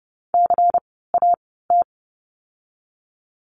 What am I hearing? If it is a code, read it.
Morse code, heard as a keyed tone, about 30 wpm, spaced as for 17 wpm, 700 Hz. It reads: CAT
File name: Lesson 16f - Letter c - Words 30wpm